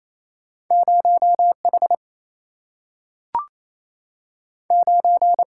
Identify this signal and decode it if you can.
Morse code, heard as a keyed tone, about 28 wpm, 700 Hz. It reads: 0 H 9
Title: Single-Letter Single-Number 28wpm